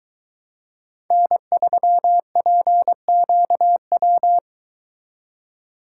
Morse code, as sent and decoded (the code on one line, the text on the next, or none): -. ...-- .--. --.- .--
N3PQW